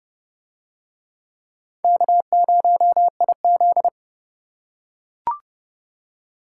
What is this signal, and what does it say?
Morse code, heard as a keyed tone, about 28 wpm, 700 Hz. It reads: K0IZ